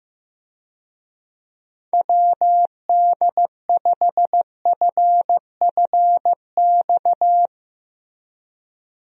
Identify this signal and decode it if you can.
Morse code, heard as a keyed tone, about 15 wpm, 700 Hz. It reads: WD5FFX
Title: USA Call Signs - 6 Characters - Repeated 3x - 15wpm